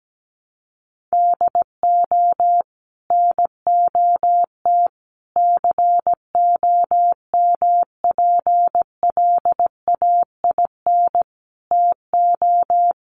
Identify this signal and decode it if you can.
Morse code, heard as a keyed tone, about 17 wpm, 700 Hz. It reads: DO NOT COMPLAIN TO
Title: Sentences from Top 1300 Words 17wpm